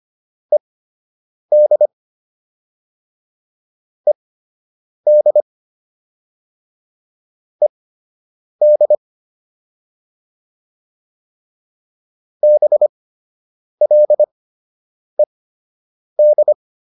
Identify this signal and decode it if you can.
Morse code, heard as a keyed tone, about 25 wpm, spaced as for 8 wpm, 600 Hz. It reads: ED ED ED BLED
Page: fundamental-practice-files